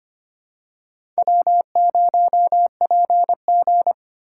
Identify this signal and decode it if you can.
Morse code, heard as a keyed tone, about 25 wpm, 700 Hz. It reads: W0PG
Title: USA Call Signs - 8x Character Spacing - 25wpm